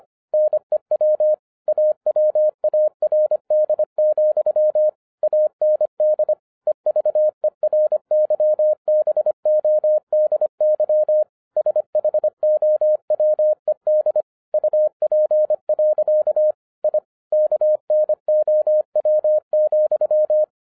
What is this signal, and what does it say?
Morse code, heard as a keyed tone, about 26 wpm, 610 Hz. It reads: NEW AWARD, AND EVERYBODY SHOWED UP. I KNOW,
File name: 2012-01-mp3 / n14-FUN WITH NUMBERS-25wpm